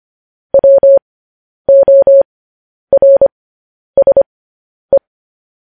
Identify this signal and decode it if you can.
Morse code, heard as a keyed tone, about 25 wpm, spaced as for 10 wpm, 550 Hz. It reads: WORSE
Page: News Headlines in Morse Code at 10 WPM